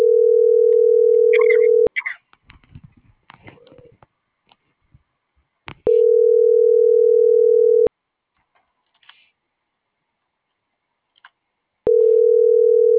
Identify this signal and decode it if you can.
Ringback tone (North America), 440 + 480 Hz.